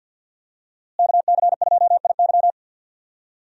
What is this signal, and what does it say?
Morse code, heard as a keyed tone, about 52 wpm, 700 Hz. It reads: KC2IY
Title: USA Call Signs - 5 Characters - Repeated 3x - 50wpm